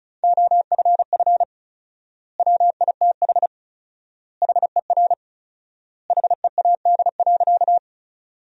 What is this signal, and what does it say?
Morse code, heard as a keyed tone, about 36 wpm, 700 Hz. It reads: OFF WITH HER HEAD.
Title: 4x Word-Spacing - Sentences from Top 300 Words - 35wpm